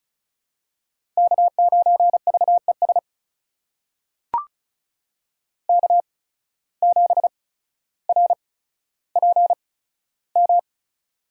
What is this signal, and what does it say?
Morse code, heard as a keyed tone, about 34 wpm, spaced as for 15 wpm, 700 Hz. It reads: K9VES K7RPM
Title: USA Call Signs - 5 Characters - 8x Character Spacing - 35wpm